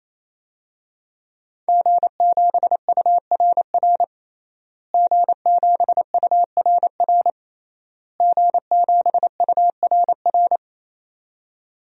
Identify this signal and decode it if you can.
Morse code, heard as a keyed tone, about 28 wpm, 690 Hz. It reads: G7URR G7URR G7URR
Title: DXCC-100 Call Signs - 5 Characters - Repeated 3x - 28wpm